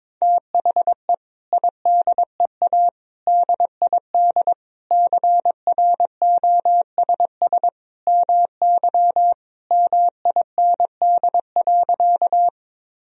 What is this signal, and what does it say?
Morse code, heard as a keyed tone, about 21 wpm, 700 Hz. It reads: THE IDEA DID CROSS MY MIND.
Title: Sentences from Top 1000 Words - Review Part 1 - 22wpm